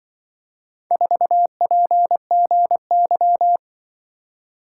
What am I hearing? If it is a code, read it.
Morse code, heard as a keyed tone, about 24 wpm, 700 Hz. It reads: 4PGY
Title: Speed-Racing - Four Letter-Number Combinations - 20wpm